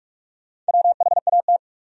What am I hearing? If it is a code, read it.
Morse code, heard as a keyed tone, about 58 wpm, 700 Hz. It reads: WHAT